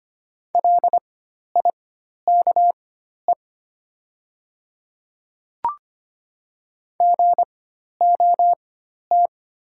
Morse code, heard as a keyed tone, about 25 wpm, spaced as for 12 wpm, 700 Hz. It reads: LIKE GOT
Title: 4x Character Spacing - Top 300 Words - 25wpm